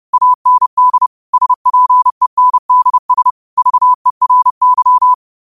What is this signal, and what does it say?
Morse code, heard as a keyed tone, about 30 wpm, 1000 Hz. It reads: AND SPENDS VERY
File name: qotdfiles / 30wpm